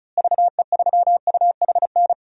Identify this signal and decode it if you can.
Morse code, heard as a keyed tone, about 36 wpm, 700 Hz. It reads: VE3UHN
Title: DXCC-100 Call Signs - 6 Characters - Repeated 2x - 35wpm